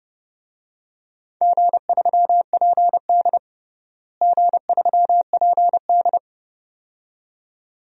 Morse code, heard as a keyed tone, about 32 wpm, 700 Hz. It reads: G3PD G3PD
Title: DXCC-100 Call Signs - 4 Characters - Repeated 2x - 30wpm